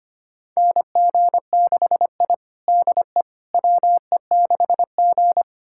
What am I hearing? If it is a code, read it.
Morse code, heard as a keyed tone, about 25 wpm, 700 Hz. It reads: NG6I DE WE6G
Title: USA Callsign DE Callsign - 4 Characters - 25wpm